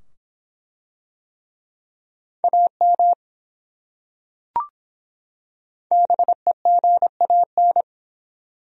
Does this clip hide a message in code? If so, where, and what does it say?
Yes: Morse code from 2.4 to 7.8 s, heard as a keyed tone, about 24 wpm, 700 Hz. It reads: AM BEGAN